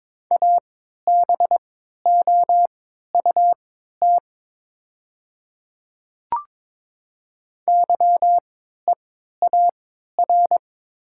Morse code, heard as a keyed tone, about 22 wpm, spaced as for 13 wpm, 690 Hz. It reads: ABOUT YEAR
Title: 3x Character Spacing - Top 200 Words - 22wpm